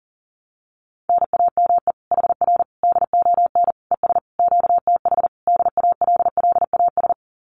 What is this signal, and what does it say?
Morse code, heard as a keyed tone, about 40 wpm, 700 Hz. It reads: NAME HR DON ES QTH DALLAS
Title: Common QSO Phrases Encore 40wpm